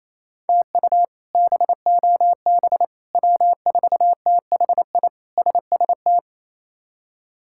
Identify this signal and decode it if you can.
Morse code, heard as a keyed tone, about 28 wpm, 700 Hz. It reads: TU BOB W4THI SST
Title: K1USN SST QSO Practice - 28wpm